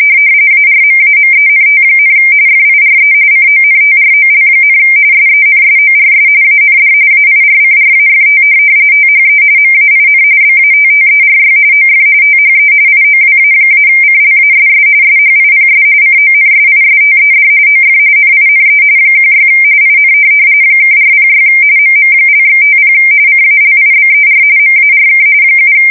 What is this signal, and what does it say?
RTTY, about 45.45 baud. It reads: ARMAMENT, FIRE CONTROL, SENSORS AND AIRCRAFT THE MAIN BATTERY OF THE IOWA-CLASS SHIPS CONSISTED OF NINE 16 IN (406 MM)/50 CALIBER MARK 7
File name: fsk170 / USS-Missouri